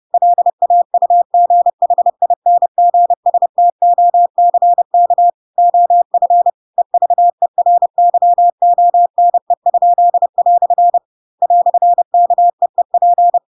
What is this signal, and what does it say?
Morse code, heard as a keyed tone, about 30 wpm, 700 Hz. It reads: LAUGHINGSTOCK OF EVERYONE?" "KEEP
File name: Pinocchio / Pinocchio ch4 30 30 ik1hge